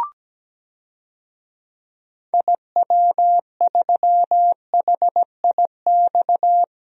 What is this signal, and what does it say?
Morse code, heard as a keyed tone, about 17 wpm, 700 Hz. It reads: IW3HIX
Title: DXCC-100 Call Signs - 6 Characters - 17wpm